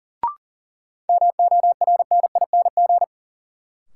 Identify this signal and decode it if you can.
Morse code, heard as a keyed tone, about 40 wpm, 700 Hz. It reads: MORNING
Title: Sentences from Top 400 Words - New Word Drill - 40wpm